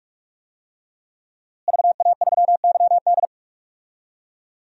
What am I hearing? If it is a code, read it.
Morse code, heard as a keyed tone, about 45 wpm, 700 Hz. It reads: VA3YD
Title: DXCC-100 Call Signs - 5 Characters - Repeated 2x - 45wpm